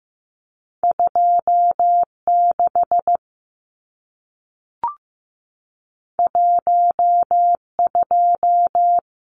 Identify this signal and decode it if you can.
Morse code, heard as a keyed tone, about 15 wpm, 700 Hz. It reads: 26 12
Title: Numbers - 2-Digit 15wpm